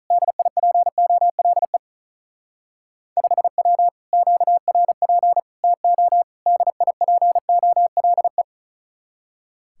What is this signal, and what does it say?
Morse code, heard as a keyed tone, about 42 wpm, 700 Hz. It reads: DIPOLE 5W QRP TO DIPOLE
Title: Speed-Racing Common QSO Phrases - 35wpm